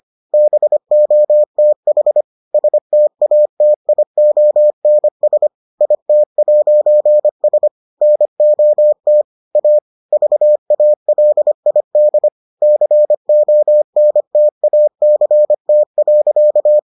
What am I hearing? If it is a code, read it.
Morse code, heard as a keyed tone, about 25 wpm, 600 Hz. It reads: BOTH STATIONS IT'S NOT A VALID CONTACT.